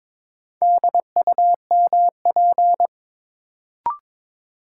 Morse code, heard as a keyed tone, about 22 wpm, 700 Hz. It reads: DUMP
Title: Sentences from Top 1300 Words - New Word Drill Alt - 22wpm